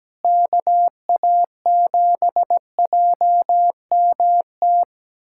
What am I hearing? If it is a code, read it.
Morse code, heard as a keyed tone, about 17 wpm, 700 Hz. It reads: KA7JMT
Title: USA Callsign DE Callsign 17wpm